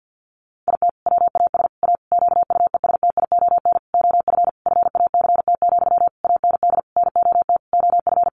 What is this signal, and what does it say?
Morse code, heard as a keyed tone, about 50 wpm, 700 Hz. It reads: IT WAS A QUESTION OF FACT, AND NOT OF
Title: Sentences from Top 1400 Words 50wpm